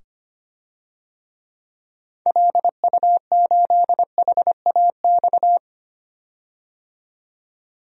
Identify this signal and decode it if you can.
Morse code, heard as a keyed tone, about 25 wpm, 700 Hz. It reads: LU8HAX
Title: DXCC-100 Call Signs - 6 Characters - 25wpm